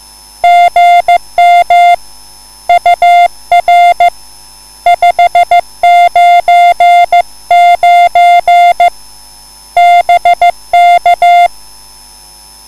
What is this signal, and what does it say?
Morse code, heard as a keyed tone, about 15 wpm, 700 Hz. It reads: GM UR 599 BK
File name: cwbook